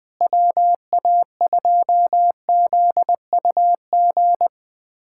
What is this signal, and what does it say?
Morse code, heard as a keyed tone, about 21 wpm, 700 Hz. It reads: WA2ZUG
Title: USA Callsign DE Callsign - 6 Characters - 20wpm